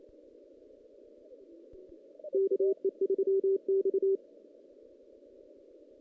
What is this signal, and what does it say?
Morse code, heard as a keyed tone, about 28 wpm, 390 Hz. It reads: KE3X
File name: 2016 JAN CW NAQP / 2016 Jan naqp k5ot - 0-5